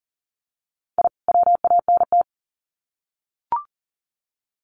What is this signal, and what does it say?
Morse code, heard as a keyed tone, about 40 wpm, 700 Hz. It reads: I WANT
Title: Sets of 2 Words 40wpm